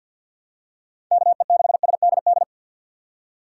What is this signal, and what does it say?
Morse code, heard as a keyed tone, about 48 wpm, 700 Hz. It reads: KE6SDD
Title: USA Call Signs - 6 Characters - 6x Character Spacing - 50wpm